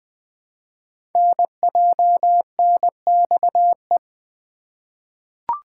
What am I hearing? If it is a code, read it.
Morse code, heard as a keyed tone, about 20 wpm, 700 Hz. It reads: NJNXE
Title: Code Groups - 5 Letters - 20wpm